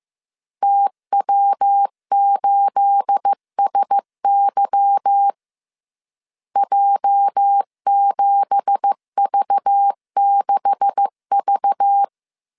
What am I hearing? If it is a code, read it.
Morse code, heard as a keyed tone, about 15 wpm, 800 Hz. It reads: TW8SY J7V6V
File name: CW / DET / 1 9 1500C S A 14wpm 3222